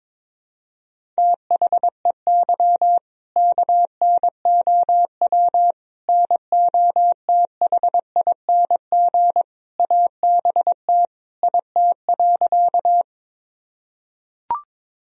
Morse code, heard as a keyed tone, about 22 wpm, 700 Hz. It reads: THEY KNOW NOTHING ABT IT.
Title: Sentences from Top 500 Words and Abbreviations 22wpm